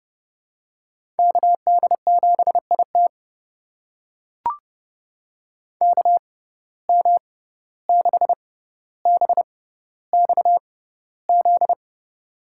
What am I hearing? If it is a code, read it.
Morse code, heard as a keyed tone, about 31 wpm, spaced as for 15 wpm, 700 Hz. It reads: KD7IT KM6BXZ
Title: USA Call Signs - 6x Character Spacing - 30wpm